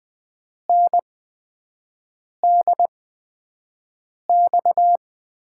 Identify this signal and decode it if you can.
Morse code, heard as a keyed tone, about 20 wpm, spaced as for 5 wpm, 700 Hz. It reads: NDX
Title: USA Call Signs - 8x Character Spacing - 20wpm